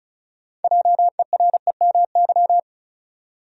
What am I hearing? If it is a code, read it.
Morse code, heard as a keyed tone, about 36 wpm, 700 Hz. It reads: JEREMY
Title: USA Common Names - 35wpm